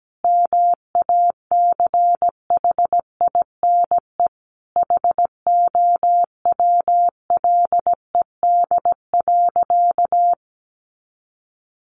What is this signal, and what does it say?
Morse code, heard as a keyed tone, about 17 wpm, 700 Hz. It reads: MACHINE HOWLED.
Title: A Sound Of Thunder 17wpm